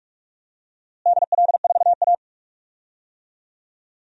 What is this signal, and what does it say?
Morse code, heard as a keyed tone, about 44 wpm, 700 Hz. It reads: DL4A
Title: DXCC-100 Call Signs - 4 Characters - Repeated 2x - 45wpm